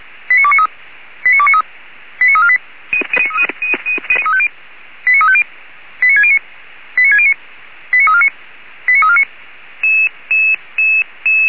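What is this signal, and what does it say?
Five-tone selective call (ZVEI), heard as quick runs of five tones, tones of about 70 ms. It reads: 89293 89293 89348 89350 89609 89609 89349 89349
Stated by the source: ZVEI five-tone call 89248, received by SDR.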